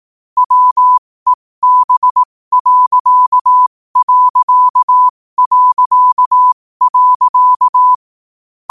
Morse code, heard as a keyed tone, about 18 wpm, spaced as for 15 wpm, 1000 Hz. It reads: WEB....
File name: qotdfiles / 15wpm